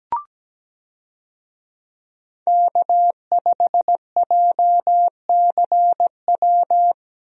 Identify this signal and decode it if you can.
Morse code, heard as a keyed tone, about 17 wpm, 700 Hz. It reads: K5JCW
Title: USA Call Signs - 5 Characters - Repeated 3x - 17wpm